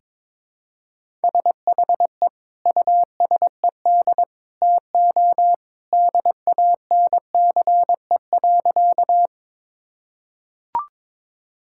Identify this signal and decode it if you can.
Morse code, heard as a keyed tone, about 22 wpm, 700 Hz. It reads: SHE USED TO DANCE.
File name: practice / 22wpm / Sentences with 2-Word Phrases - 22wpm